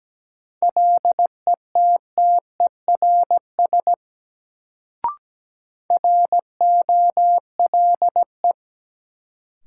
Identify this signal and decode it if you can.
Morse code, heard as a keyed tone, about 17 wpm, 700 Hz. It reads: LETTERS ROLE
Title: Sentences from Top 1000 Words - Review Part 6 - New Word Drill - 17wpm